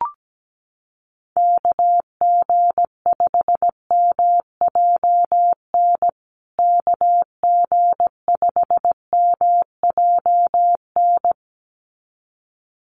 Morse code, heard as a keyed tone, about 17 wpm, 700 Hz. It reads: KG5MJN KG5MJN
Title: USA Call Signs Repeated 2x 17wpm